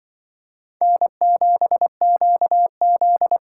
Morse code, heard as a keyed tone, about 24 wpm, 700 Hz. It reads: N7QZ
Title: Speed Racing - USA Call Signs 20wpm